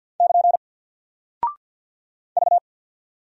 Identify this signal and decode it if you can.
Morse code, heard as a keyed tone, about 49 wpm, 700 Hz. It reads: <KN> V
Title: Single Character (Letter:Number:Punctuation:Prosign) Alt 50wpm